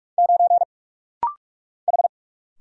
Morse code, heard as a keyed tone, about 46 wpm, 700 Hz. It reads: 9 H
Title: Single Letter-Number Alt - Rapid-Fire 45wpm